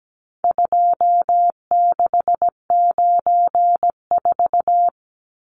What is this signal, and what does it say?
Morse code, heard as a keyed tone, about 17 wpm, 700 Hz. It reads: 2694
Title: Numbers - 4-Digit 17wpm